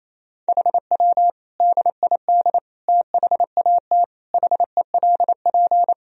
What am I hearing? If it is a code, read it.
Morse code, heard as a keyed tone, about 28 wpm, 700 Hz. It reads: HW DID THAT HELP
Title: Sentences from Top 200 Words and Abbreviations 28wpm